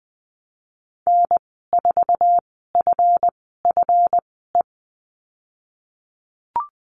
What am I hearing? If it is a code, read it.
Morse code, heard as a keyed tone, about 20 wpm, spaced as for 15 wpm, 700 Hz. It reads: N4FFE
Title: 154 Lesson 25g - Letter p - Call Signs 20wpm